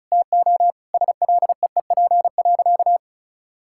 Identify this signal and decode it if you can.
Morse code, heard as a keyed tone, about 34 wpm, 700 Hz. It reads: TO SLEEP.
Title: Sentences from Top 1000 Words - Review Part 6 - 35wpm